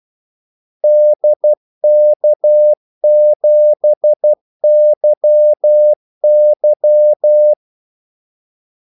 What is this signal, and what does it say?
Morse code, heard as a keyed tone, about 12 wpm, 600 Hz. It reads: DK7YY
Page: Kaldesignaler 12 wpm